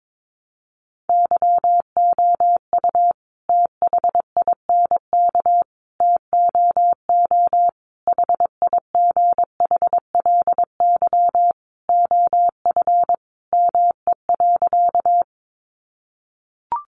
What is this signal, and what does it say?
Morse code, heard as a keyed tone, about 22 wpm, 700 Hz. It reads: YOU THINK TOO HIGHLY OF ME.
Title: Sentences from Top 1100 Words 22wpm